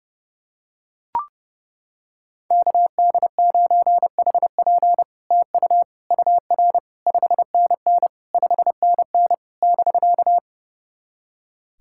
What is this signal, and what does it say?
Morse code, heard as a keyed tone, about 30 wpm, 700 Hz. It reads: KD9HP TU UR 5NN 5NN <BK>
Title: SOTA QSO Practice 30wpm